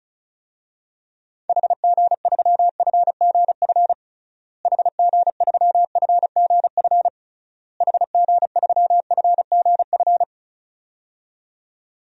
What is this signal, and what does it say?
Morse code, heard as a keyed tone, about 36 wpm, 700 Hz. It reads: HG3FGF HG3FGF HG3FGF
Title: DXCC-100 Call Signs - 6 Characters - Repeated 3x - 35wpm